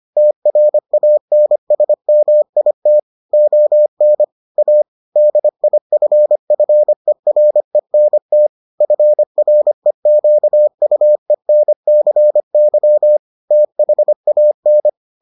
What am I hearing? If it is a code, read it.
Morse code, heard as a keyed tone, about 25 wpm, 600 Hz. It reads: TRANSMIT ON A DIFFERENT FREQUENCY THAN